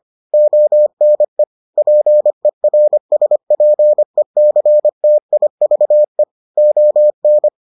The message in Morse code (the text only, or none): ONE PERSPECTIVE ON